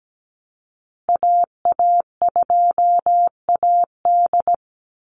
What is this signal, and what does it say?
Morse code, heard as a keyed tone, about 17 wpm, 700 Hz. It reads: AA2AD
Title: Call Signs Repeated 2x - 17wpm